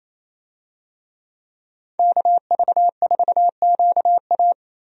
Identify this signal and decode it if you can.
Morse code, heard as a keyed tone, about 28 wpm, 700 Hz. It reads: KV4QA
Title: USA Call Signs - 5 Characters - Repeated 3x - 28wpm